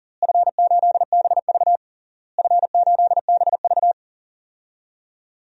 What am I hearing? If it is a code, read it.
Morse code, heard as a keyed tone, about 40 wpm, 700 Hz. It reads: F8BV F8BV
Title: DXCC-100 Call Signs - 4 Characters - Repeated 3x - 40wpm